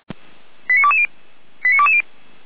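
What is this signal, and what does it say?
Five-tone selective call (ZVEI), heard as quick runs of five tones, tones of about 70 ms. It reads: 89220 89220
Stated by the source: ZVEI five-tone selective call received by SDR.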